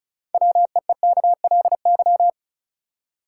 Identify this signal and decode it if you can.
Morse code, heard as a keyed tone, about 34 wpm, 700 Hz. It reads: WEEKLY
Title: Six Letter Words - 35wpm